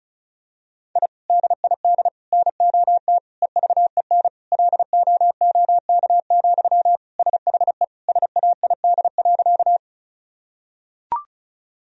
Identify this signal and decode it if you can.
Morse code, heard as a keyed tone, about 36 wpm, 700 Hz. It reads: I DID NOT EVEN LOOK, SHE SAID.
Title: Sentences from Top 1000 Words - Review Part 1 - 35wpm